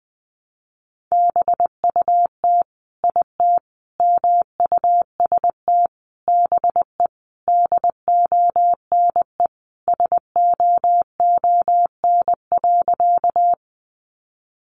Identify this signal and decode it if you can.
Morse code, heard as a keyed tone, about 20 wpm, 700 Hz. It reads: BUT IT MUST BE DONE SOON.
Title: Sentences from Top 500 Words 20wpm